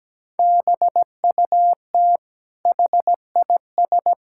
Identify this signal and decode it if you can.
Morse code, heard as a keyed tone, about 17 wpm, 700 Hz. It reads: BUT HIS
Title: Sentences from Top 1000 Words - Review Part 6 - 17wpm